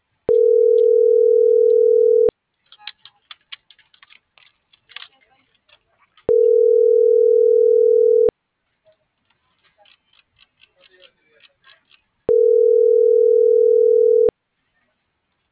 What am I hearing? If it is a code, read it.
Ringback tone (North America), 440 + 480 Hz.